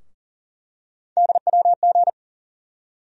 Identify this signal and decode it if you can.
Morse code, heard as a keyed tone, about 40 wpm, 700 Hz. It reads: DWG
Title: USA Call Sign Postfixes - 3 Letters 40wpm